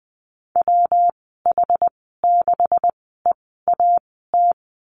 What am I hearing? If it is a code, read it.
Morse code, heard as a keyed tone, about 20 wpm, spaced as for 15 wpm, 700 Hz. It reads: WH6EAT
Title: 193 Lesson 31g - Letter v - Call Signs 20wpm